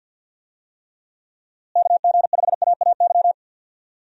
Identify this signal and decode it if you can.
Morse code, heard as a keyed tone, about 52 wpm, 700 Hz. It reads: KG5AAY
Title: USA Call Signs - 6 Characters - Repeated 3x - 50wpm